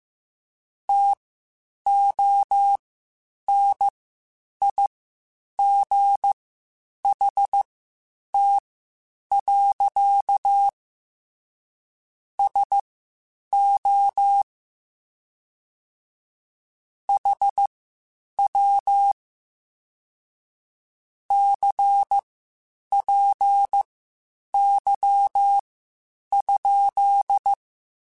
Morse code, heard as a keyed tone, about 15 wpm, spaced as for 8 wpm, 800 Hz. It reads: TONIGHT. SO HW CPY?
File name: (spaced) / QSOs Spaced-mp3s / WA7QOX 1 @13 sp